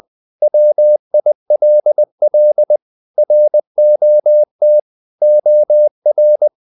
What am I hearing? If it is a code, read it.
Morse code, heard as a keyed tone, about 20 wpm, 600 Hz. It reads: WILL ROT OR
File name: the old man and the sea / the old man and the sea part 1 20wpm